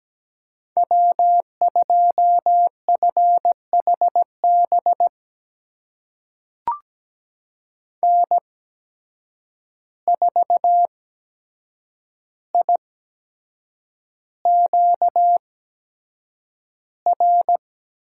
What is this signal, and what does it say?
Morse code, heard as a keyed tone, about 17 wpm, spaced as for 7 wpm, 700 Hz. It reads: W2FHB N4IQR
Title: USA Call Signs - 5 Characters - 8x Character Spacing - 17wpm